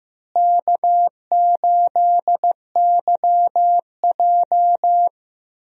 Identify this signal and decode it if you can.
Morse code, heard as a keyed tone, about 15 wpm, 700 Hz. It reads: K8YJ
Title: USA Call Signs - 4 Characters - Repeated 3x - 15wpm